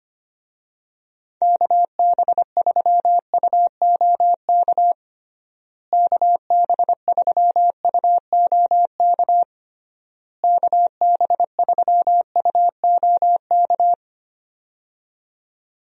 Morse code, heard as a keyed tone, about 25 wpm, 700 Hz. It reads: KB3UOK KB3UOK KB3UOK
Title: USA Call Signs - 6 Characters - Repeated 3x - 25wpm